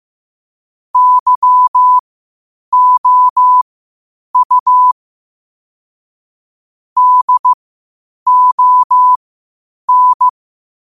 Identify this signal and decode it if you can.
Morse code, heard as a keyed tone, about 15 wpm, spaced as for 8 wpm, 1000 Hz. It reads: YOU DON
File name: qotdfiles / 8wpm